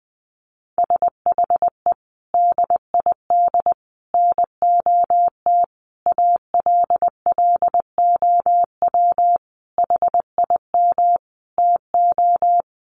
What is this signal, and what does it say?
Morse code, heard as a keyed tone, about 20 wpm, 700 Hz. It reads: SHE DID NOT ALLOW HIM TO
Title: Sentences from Top 1200 Words 20wpm